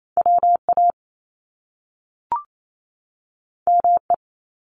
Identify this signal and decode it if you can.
Morse code, heard as a keyed tone, about 28 wpm, 700 Hz. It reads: WA ME
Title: Top Two Letter Combinations in Words 28wpm